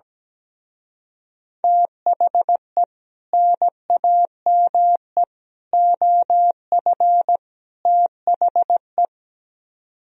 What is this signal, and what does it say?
Morse code, heard as a keyed tone, about 17 wpm, 700 Hz. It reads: THE NAME OF THE
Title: Speed-Racing - Sets of 4 Words 17wpm